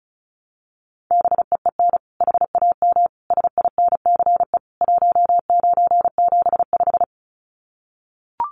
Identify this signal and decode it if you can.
Morse code, heard as a keyed tone, about 34 wpm, 700 Hz. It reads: BEEN HAM SINCE 1975
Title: Common QSO Phrases 35wpm